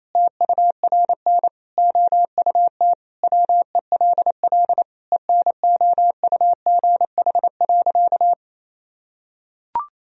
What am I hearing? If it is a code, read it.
Morse code, heard as a keyed tone, about 28 wpm, 700 Hz. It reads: TURN OUT WELL ENOUGH.
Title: Sentences from Top 1000 Words - Review Part 3 - 28wpm